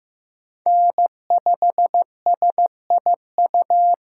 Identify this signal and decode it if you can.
Morse code, heard as a keyed tone, about 15 wpm, 700 Hz. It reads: N5SIU
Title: USA Call Signs - 8x Character Spacing - 15wpm